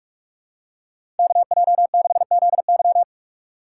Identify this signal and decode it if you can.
Morse code, heard as a keyed tone, about 46 wpm, 700 Hz. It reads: KJ6ZY
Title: USA Call Signs - 5 Characters - 2x Character Spacing - 45wpm